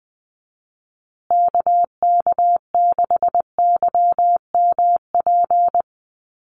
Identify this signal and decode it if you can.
Morse code, heard as a keyed tone, about 20 wpm, 700 Hz. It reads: KK6YMP
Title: USA Call Signs Repeated 3x 20wpm